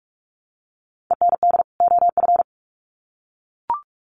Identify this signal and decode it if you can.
Morse code, heard as a keyed tone, about 44 wpm, 700 Hz. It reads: END OF
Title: Sets of 2 Words 45wpm